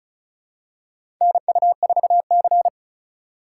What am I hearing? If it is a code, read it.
Morse code, heard as a keyed tone, about 36 wpm, 700 Hz. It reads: NU4C
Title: USA Call Signs - 4x Character Spacing - 35wpm